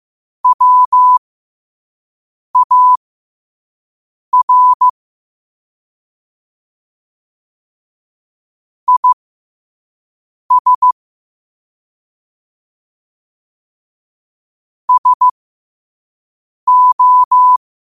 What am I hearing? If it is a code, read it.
Morse code, heard as a keyed tone, about 15 wpm, spaced as for 5 wpm, 1000 Hz. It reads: WAR IS SO
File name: qotdfiles / 5wpm